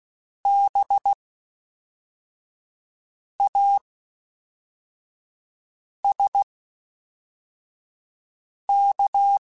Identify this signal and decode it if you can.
Morse code, heard as a keyed tone, about 16 wpm, spaced as for 4 wpm, 800 Hz. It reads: BASK
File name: Special INGs / Special INGs-mp3s / Fast ing @10